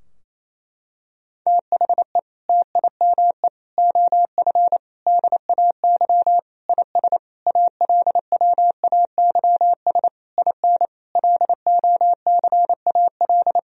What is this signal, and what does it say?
Morse code, heard as a keyed tone, about 28 wpm, 700 Hz. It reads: THE TIME OF DAY IS ALWAYS IN LOCAL
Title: Sentences from Top 1000 Words - Review Part 3 - 28wpm